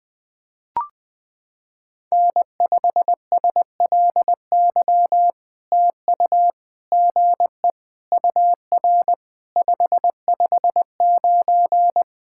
Morse code, heard as a keyed tone, about 20 wpm, 700 Hz. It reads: N5SLY TU GE UR 559
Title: POTA QSO Practice - 20wpm